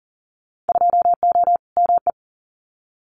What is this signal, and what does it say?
Morse code, heard as a keyed tone, about 40 wpm, 700 Hz. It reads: 2O ME
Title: Winter Field Day Exchanges 40wpm